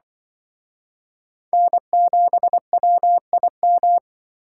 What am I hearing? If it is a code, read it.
Morse code, heard as a keyed tone, about 24 wpm, 700 Hz. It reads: N7WIM
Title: Speed Racing - USA 1x3 Call Signs 20wpm